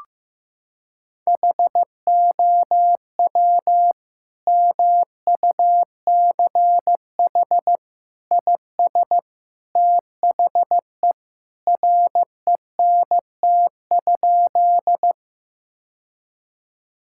Morse code, heard as a keyed tone, about 15 wpm, 700 Hz. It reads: HOW MUCH IS THE RENT?
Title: Sentences from Top 1000 Words - Review Part 5 - 15wpm